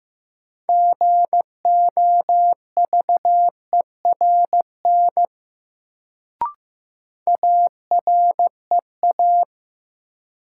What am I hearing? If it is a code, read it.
Morse code, heard as a keyed tone, about 15 wpm, 700 Hz. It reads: GOVERN AREA
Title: Sentences from Top 1000 Words - Review Part 2 - New Word Drill - 15wpm